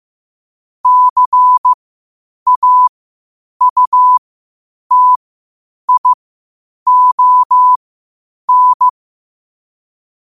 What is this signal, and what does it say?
Morse code, heard as a keyed tone, about 15 wpm, spaced as for 8 wpm, 1000 Hz. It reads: CAUTION
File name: qotdfiles / 8wpm